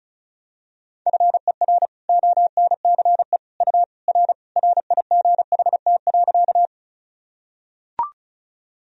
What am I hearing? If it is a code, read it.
Morse code, heard as a keyed tone, about 34 wpm, 700 Hz. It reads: FER ONCE U R RIGHT.
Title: Sentences from Top 300 Words and Abbreviations 35wpm